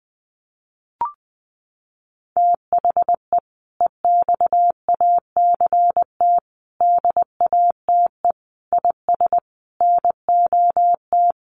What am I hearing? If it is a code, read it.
Morse code, heard as a keyed tone, about 20 wpm, 700 Hz. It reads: THE EXACT DATE IS NOT
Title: Sentences from Top 1300 Words 20wpm